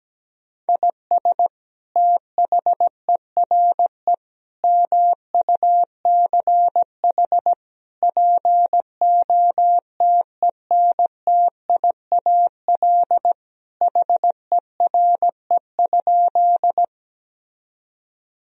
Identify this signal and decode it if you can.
Morse code, heard as a keyed tone, about 17 wpm, 700 Hz. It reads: IS THERE MUCH POTENTIAL HERE?
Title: Sentences from Top 1000 Words - Review Part 4 - 17wpm